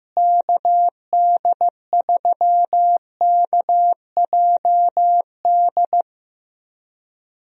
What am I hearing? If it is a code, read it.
Morse code, heard as a keyed tone, about 15 wpm, 700 Hz. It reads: KD3KJD
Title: USA Call Signs - 6 Characters - Repeated 2x - 15wpm